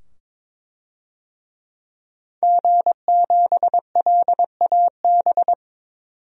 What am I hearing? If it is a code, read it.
Morse code, heard as a keyed tone, about 22 wpm, 700 Hz. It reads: G7LAB
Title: DXCC-100 Call Signs - 5 Characters - 22wpm